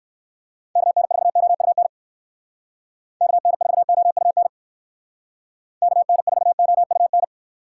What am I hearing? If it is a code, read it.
Morse code, heard as a keyed tone, about 68 wpm, 700 Hz. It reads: KN4OUN KN4OUN KN4OUN